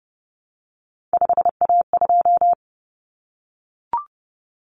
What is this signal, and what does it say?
Morse code, heard as a keyed tone, about 30 wpm, 700 Hz. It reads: 5A2
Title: Common Cut Numbers - 3-Digit 30wpm